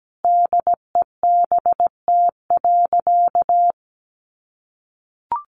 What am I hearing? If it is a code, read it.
Morse code, heard as a keyed tone, about 17 wpm, 700 Hz. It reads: DEBT.
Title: Sentences from Top 1300 Words 17wpm